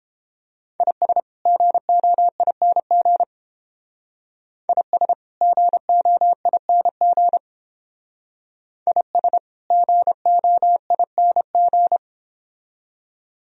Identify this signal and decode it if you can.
Morse code, heard as a keyed tone, about 30 wpm, 700 Hz. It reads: IS GOING IS GOING IS GOING